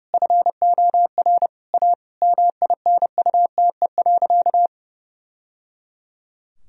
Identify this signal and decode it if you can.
Morse code, heard as a keyed tone, about 30 wpm, 700 Hz. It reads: FOR A MINUTE.
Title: Sentences from Top 1000 Words - Review Part 2 - 30wpm